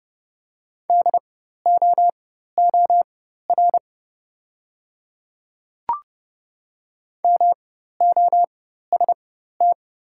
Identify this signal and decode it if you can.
Morse code, heard as a keyed tone, about 32 wpm, spaced as for 14 wpm, 700 Hz. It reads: DOOR MOST
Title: 4x Character Spacing - Top 300 Words - 30wpm